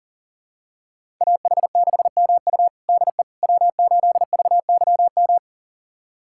Morse code, heard as a keyed tone, about 37 wpm, 700 Hz. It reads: AH6MU DE W8VYM
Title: USA Callsign DE Callsign - 5 Characters - 40wpm